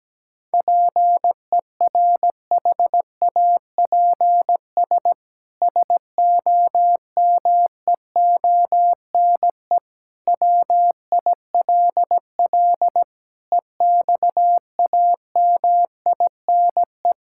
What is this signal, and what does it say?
Morse code, heard as a keyed tone, about 17 wpm, 700 Hz. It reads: PERHAPS SOMEONE WILL EXAMINE
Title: Sentences from Top 1600 Words 17wpm